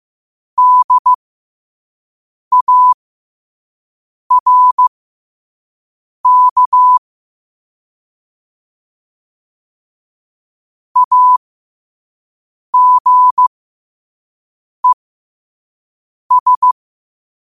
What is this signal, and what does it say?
Morse code, heard as a keyed tone, about 15 wpm, spaced as for 5 wpm, 1000 Hz. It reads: DARK AGES
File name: qotdfiles / 5wpm